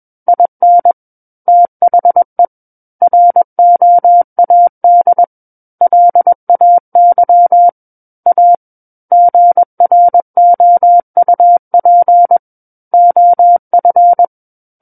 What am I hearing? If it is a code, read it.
Morse code, heard as a keyed tone, about 21 wpm, 700 Hz. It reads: IN THE ROAD LAY A GROUP OF
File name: Book One / 12-Chapter 12 - 21 WPM